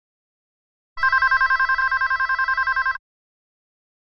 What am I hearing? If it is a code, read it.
There is no code.